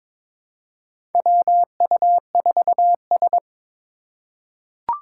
Morse code, heard as a keyed tone, about 22 wpm, 700 Hz. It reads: WU4S
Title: USA Call Signs - 4 Characters - 2x Character Spacing - 22wpm